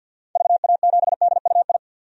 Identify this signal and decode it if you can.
Morse code, heard as a keyed tone, about 46 wpm, 700 Hz. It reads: VA7DUI
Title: DXCC-100 Call Signs - 6 Characters - 50wpm